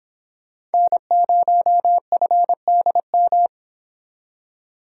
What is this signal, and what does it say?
Morse code, heard as a keyed tone, about 31 wpm, 700 Hz. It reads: N0FDM